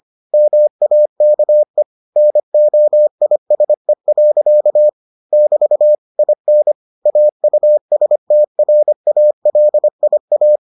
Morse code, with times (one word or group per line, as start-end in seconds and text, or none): 0.3-1.8 MAKE
2.2-4.9 NOISE.
5.3-5.9 <BT>
6.2-6.7 IN
7.1-10.6 AUSTRALIA